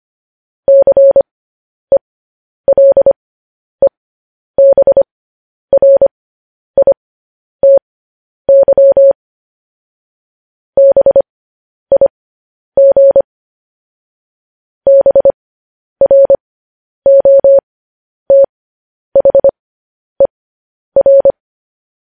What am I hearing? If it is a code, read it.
Morse code, heard as a keyed tone, about 25 wpm, spaced as for 10 wpm, 550 Hz. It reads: CELEBRITY BIG BROTHER
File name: podcast / News headlines-1x-25x10